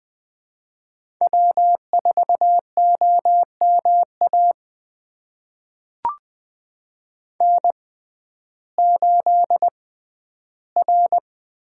Morse code, heard as a keyed tone, about 20 wpm, spaced as for 12 wpm, 700 Hz. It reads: W4OMA N8R
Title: USA Call Signs - 5 Characters - 6x Character Spacing - 20wpm